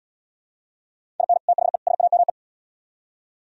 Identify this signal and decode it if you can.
Morse code, heard as a keyed tone, about 71 wpm, 700 Hz. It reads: IN THE SAME